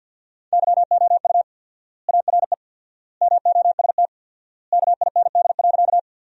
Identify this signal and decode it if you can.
Morse code, heard as a keyed tone, about 50 wpm, 700 Hz. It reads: YOU ARE MOST KIND.
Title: 4x Word-Spacing - Sentences from Top 200 Words - 50wpm